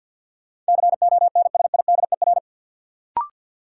Morse code, heard as a keyed tone, about 48 wpm, 700 Hz. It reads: CONSIDER
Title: Sentences from Top 600 Words - New Word Drill - 50wpm